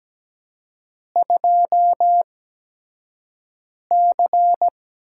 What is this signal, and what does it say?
Morse code, heard as a keyed tone, about 17 wpm, spaced as for 5 wpm, 700 Hz. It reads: 2C
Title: USA Call Signs - 6 Characters - 8x Character Spacing - 17wpm